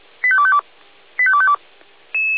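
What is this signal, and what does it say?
Five-tone selective call (ZVEI), heard as quick runs of five tones, tones of about 70 ms. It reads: 85252 85252
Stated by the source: ZVEI five-tone call received by SDR.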